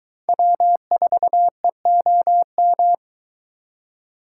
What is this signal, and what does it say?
Morse code, heard as a keyed tone, about 22 wpm, 700 Hz. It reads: W4EOM